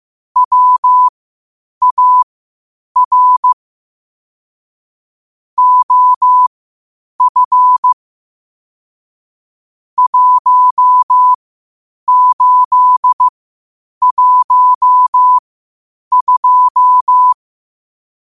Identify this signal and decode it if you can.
Morse code, heard as a keyed tone, about 15 wpm, spaced as for 8 wpm, 1000 Hz. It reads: WAR OF 1812
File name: qotdfiles / 8wpm